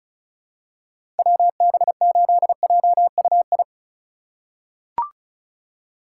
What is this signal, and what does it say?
Morse code, heard as a keyed tone, about 34 wpm, 700 Hz. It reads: WB8JUI
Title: K1USN SST QSO Practice - 35wpm